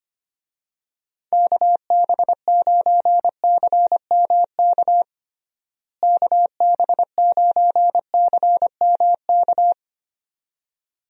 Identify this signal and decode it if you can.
Morse code, heard as a keyed tone, about 25 wpm, 700 Hz. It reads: KB9CMK KB9CMK
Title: USA Call Signs - 6 Characters - Repeated 2x - 25wpm